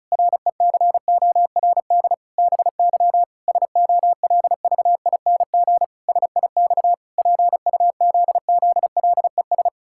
Morse code, heard as a keyed tone, about 36 wpm, 700 Hz. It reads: RECORD BY SOLVING SIX PUZZLES
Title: Pangram Challenge v2 - 35wpm